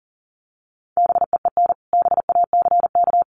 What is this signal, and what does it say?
Morse code, heard as a keyed tone, about 40 wpm, 700 Hz. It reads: BEEN BACK
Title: Sentences from Top 200 Words 40wpm